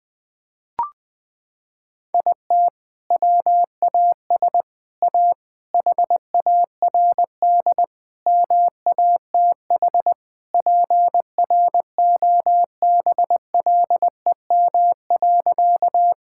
Morse code, heard as a keyed tone, about 20 wpm, 700 Hz. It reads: IT WAS A HARD MATH PROBLEM.
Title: Sentences from Top 1000 Words - Review Part 6 - 20wpm